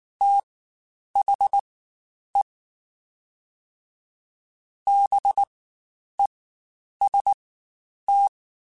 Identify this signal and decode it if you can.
Morse code, heard as a keyed tone, about 19 wpm, spaced as for 9 wpm, 800 Hz. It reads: THE BEST
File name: Quotes Spaced / Quotes Spaced-mp3s / Temper sp @18